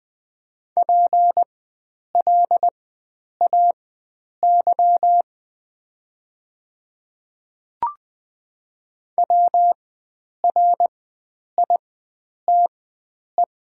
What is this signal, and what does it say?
Morse code, heard as a keyed tone, about 20 wpm, spaced as for 9 wpm, 690 Hz. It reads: PLAY WRITE
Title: 4x Character Spacing - Top 300 Words - 20wpm